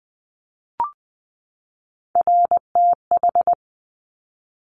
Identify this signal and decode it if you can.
Morse code, heard as a keyed tone, about 20 wpm, 700 Hz. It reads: RTH
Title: Top Three Letter Combinations in Words 20wpm